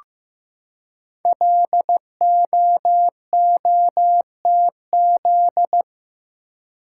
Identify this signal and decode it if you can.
Morse code, heard as a keyed tone, about 15 wpm, 700 Hz. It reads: LOOTZ
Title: Code Groups - 5 Letters - 15wpm